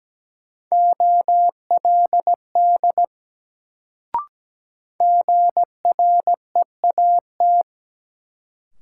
Morse code, heard as a keyed tone, about 17 wpm, 700 Hz. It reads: OLD GREAT
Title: Sentences from Top 200 Words - New Word Drill - 17wpm